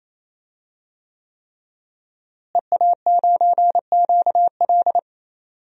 Morse code, heard as a keyed tone, about 28 wpm, 690 Hz. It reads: EA9QL
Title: DXCC-100 Call Signs - 5 Characters - Repeated 3x - 28wpm